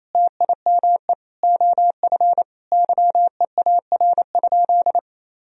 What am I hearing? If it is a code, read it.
Morse code, heard as a keyed tone, about 28 wpm, 700 Hz. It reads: TIME OF YEAR?
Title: Sentences from Top 1000 Words - Review Part 3 - 28wpm